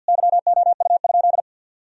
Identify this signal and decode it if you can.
Morse code, heard as a keyed tone, about 50 wpm, 700 Hz. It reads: YOU?
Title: Sentences from Top 1000 Words - Review Part 2 - 50wpm